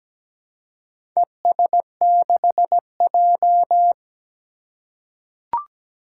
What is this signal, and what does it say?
Morse code, heard as a keyed tone, about 17 wpm, 700 Hz. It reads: ES6J
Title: Four Challenging Letter-Number Combinations 17wpm